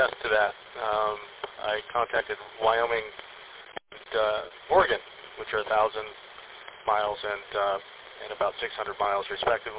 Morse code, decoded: none

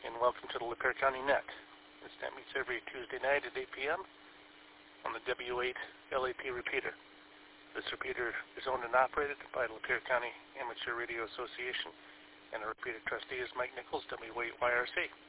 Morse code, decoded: none

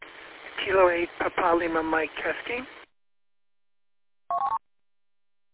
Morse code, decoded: none

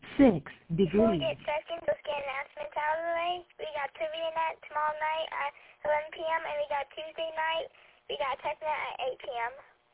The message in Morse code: none

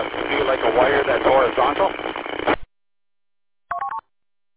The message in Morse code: none

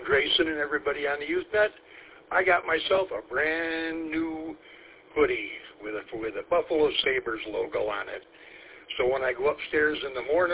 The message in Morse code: none

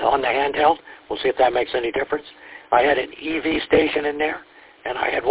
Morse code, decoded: none

